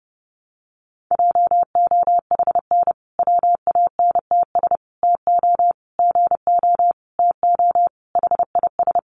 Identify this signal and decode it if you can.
Morse code, heard as a keyed tone, about 30 wpm, 700 Hz. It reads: JOHN WANTS TO GO TO HIS
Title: Additional Easy Phrases 30wpm